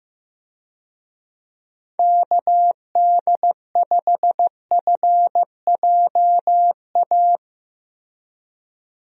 Morse code, heard as a keyed tone, about 15 wpm, 700 Hz. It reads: KD5FJA
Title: USA Call Signs - 6 Characters - Repeated 2x - 15wpm